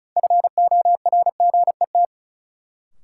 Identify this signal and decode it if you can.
Morse code, heard as a keyed tone, about 36 wpm, 700 Hz. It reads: FORGET
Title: Sentences from Top 1200 Words - New Word Drill - 35wpm